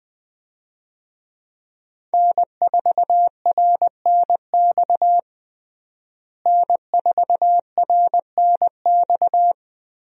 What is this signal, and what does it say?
Morse code, heard as a keyed tone, about 20 wpm, 700 Hz. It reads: N4RNX N4RNX
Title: USA Call Signs - 5 Characters - Repeated 2x - 20wpm